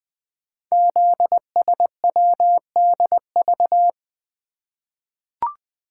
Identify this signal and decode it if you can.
Morse code, heard as a keyed tone, about 20 wpm, 700 Hz. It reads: ZSWDV
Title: Code Groups - 5 Letters - 20wpm